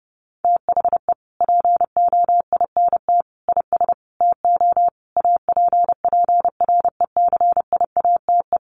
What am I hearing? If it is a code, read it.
Morse code, heard as a keyed tone, about 28 wpm, 700 Hz. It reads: THE POINT IS TO APPRECIATE
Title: Sentences from Top 1300 Words 30wpm